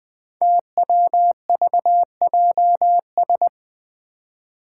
Morse code, heard as a keyed tone, about 20 wpm, 700 Hz. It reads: TWVJS
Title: Five Challenging Letter-Number Combinations 20wpm